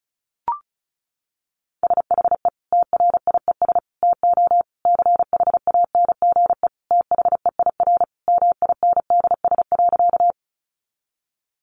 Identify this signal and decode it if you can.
Morse code, heard as a keyed tone, about 34 wpm, 700 Hz. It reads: SHE TRIES TO CHANGE THEIR MINDS.
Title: Easy Phrases 35wpm